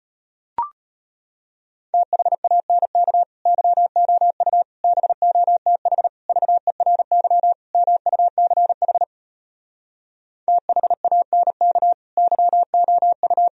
Morse code, heard as a keyed tone, about 33 wpm, 700 Hz. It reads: THANK YOU BOTH VERY MUCH THANK YOU